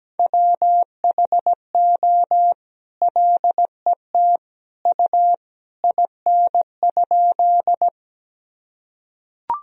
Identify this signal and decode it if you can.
Morse code, heard as a keyed tone, about 17 wpm, 700 Hz. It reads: WHO LET U IN?
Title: Sentences from Top 300 Words and Abbreviations 17wpm